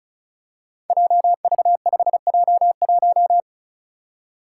Morse code, heard as a keyed tone, about 36 wpm, 700 Hz. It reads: JV5J1
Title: Five Challenging Letter-Number Combinations 35wpm